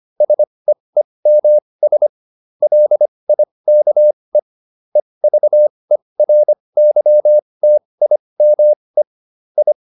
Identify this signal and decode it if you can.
Morse code, heard as a keyed tone, about 25 wpm, spaced as for 20 wpm, 600 Hz. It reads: SEEMS LIKE EVERYTIME I